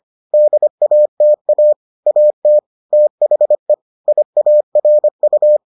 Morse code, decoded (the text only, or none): DATA AT THE IARU